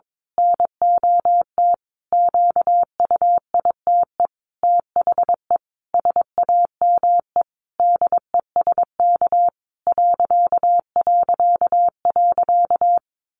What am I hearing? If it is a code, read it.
Morse code, heard as a keyed tone, about 22 wpm, 700 Hz. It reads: NOT QUITE THE SAME DESK ...
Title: A Sound Of Thunder 22wpm